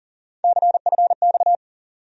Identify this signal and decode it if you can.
Morse code, heard as a keyed tone, about 40 wpm, 700 Hz. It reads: CFX
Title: USA Call Sign Postfixes - 3 Letters 40wpm